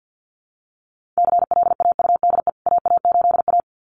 Morse code, heard as a keyed tone, about 50 wpm, 700 Hz. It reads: CLAUDE AA8A
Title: CWT Exchange 50wpm